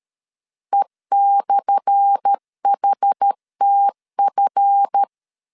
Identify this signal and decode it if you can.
Morse code, heard as a keyed tone, about 13 wpm, 800 Hz. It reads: E/HTF
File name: LEZIONI / 10 SIGNS / 10A SLASH 12wpm 2314